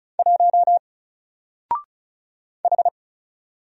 Morse code, heard as a keyed tone, about 35 wpm, 700 Hz. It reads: 1 H
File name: Single-Letter-Number-35wpm